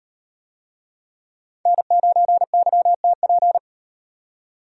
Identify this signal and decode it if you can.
Morse code, heard as a keyed tone, about 38 wpm, 700 Hz. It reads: N9YTP